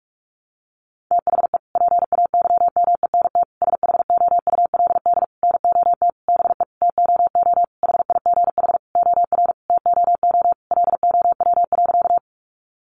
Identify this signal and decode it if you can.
Morse code, heard as a keyed tone, about 44 wpm, 700 Hz. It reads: THE PAYMENT SHOULD NOT BE TOO HIGH OR TOO LOW.
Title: Sentences from Top 1200 Words 45wpm